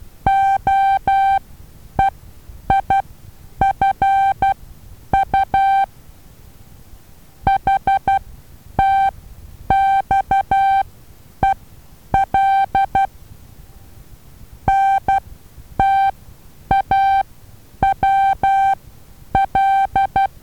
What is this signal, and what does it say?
Morse code, heard as a keyed tone, about 12 wpm, spaced as for 9 wpm, 780 Hz. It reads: OEIFU HTXEL NTAWL
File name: Taviro HA6VA B